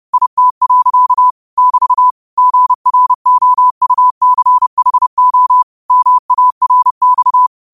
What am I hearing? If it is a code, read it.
Morse code, heard as a keyed tone, about 30 wpm, 1000 Hz. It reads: IT. <BT> GROUCHO MARX
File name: qotdfiles / 30wpm